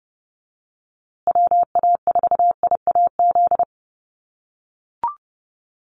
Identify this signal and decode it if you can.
Morse code, heard as a keyed tone, about 30 wpm, 700 Hz. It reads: WA4IAZ
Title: USA Call Signs Repeated 3x 30wpm